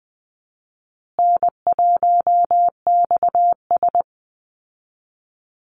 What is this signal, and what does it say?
Morse code, heard as a keyed tone, about 20 wpm, 700 Hz. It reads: N1XS
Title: Easy Call Signs Repeated 2x - 20wpm